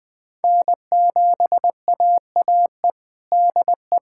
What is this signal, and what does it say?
Morse code, heard as a keyed tone, about 20 wpm, 700 Hz. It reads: N7AAE DE
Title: USA Callsign DE Callsign 20wpm